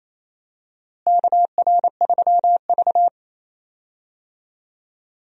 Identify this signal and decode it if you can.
Morse code, heard as a keyed tone, about 28 wpm, 700 Hz. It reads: KR3V
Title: USA Call Signs - 4 Characters - Repeated 2x - 28wpm